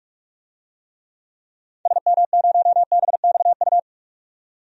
Morse code, heard as a keyed tone, about 46 wpm, 700 Hz. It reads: SM0BXU
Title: DXCC-100 Call Signs - 6 Characters - 45wpm